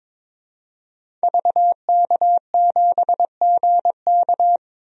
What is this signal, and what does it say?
Morse code, heard as a keyed tone, about 22 wpm, 700 Hz. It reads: VK7GK
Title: DXCC-100 Call Signs - 5 Characters - 22wpm